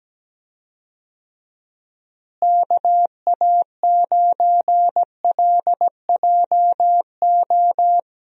Morse code, heard as a keyed tone, about 17 wpm, 700 Hz. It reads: KA9LJO
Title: USA Call Signs - 6 Characters - Repeated 2x - 17wpm